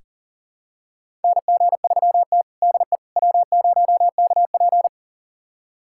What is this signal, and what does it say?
Morse code, heard as a keyed tone, about 40 wpm, 700 Hz. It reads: NG3T DE W0KP
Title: USA Callsign DE Callsign - 4 Characters - 40wpm